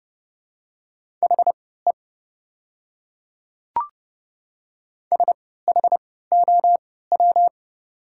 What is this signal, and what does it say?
Morse code, heard as a keyed tone, about 29 wpm, spaced as for 17 wpm, 700 Hz. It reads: HE SHOW
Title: 3x Character Spacing - Top 300 Words - 30wpm